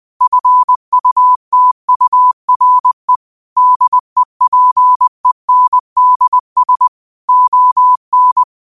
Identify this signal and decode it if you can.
Morse code, heard as a keyed tone, about 20 wpm, 1000 Hz. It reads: FUTURE DEPENDS ON
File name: qotdfiles / 20wpm